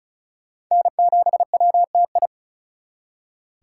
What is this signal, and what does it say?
Morse code, heard as a keyed tone, about 39 wpm, 700 Hz. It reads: N7WTI
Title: Speed Racing - USA 1x3 Call Signs 35wpm